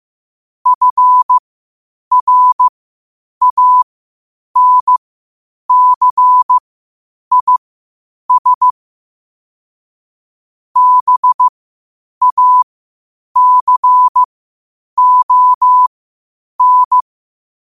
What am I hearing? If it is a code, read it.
Morse code, heard as a keyed tone, about 15 wpm, spaced as for 8 wpm, 1000 Hz. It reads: FRANCIS BACON
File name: qotdfiles / 8wpm